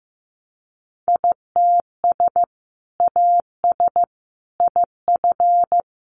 Morse code, heard as a keyed tone, about 15 wpm, 700 Hz. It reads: ITS AS IF
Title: Sentences from Top 800 Words 15wpm